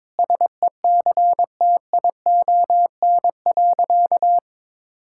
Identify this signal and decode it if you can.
Morse code, heard as a keyed tone, about 22 wpm, 700 Hz. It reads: SECTION.
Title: Sentences from Top 1000 Words - Review Part 5 - 22wpm